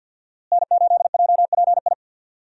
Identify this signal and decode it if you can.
Morse code, heard as a keyed tone, about 47 wpm, 700 Hz. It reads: N8JPI
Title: USA Call Signs - 5 Characters - Repeated 2x - 50wpm